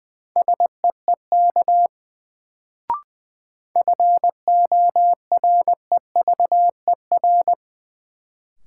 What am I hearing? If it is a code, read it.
Morse code, heard as a keyed tone, about 21 wpm, 700 Hz. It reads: SEEK FOREVER
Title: Sentences from Top 1600 Words - New Word Drill Alt - 20wpm